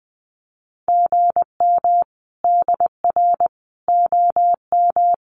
Morse code, heard as a keyed tone, about 20 wpm, 700 Hz. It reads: GM DR OM
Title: Common QSO Phrases 20wpm